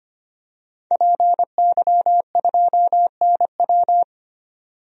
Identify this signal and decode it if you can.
Morse code, heard as a keyed tone, about 25 wpm, 700 Hz. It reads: PY2NW
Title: DXCC-100 Call Signs - 5 Characters - 25wpm